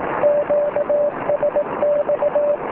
Morse code, heard as a keyed tone, about 18 wpm, 590 Hz, with noise. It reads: QSX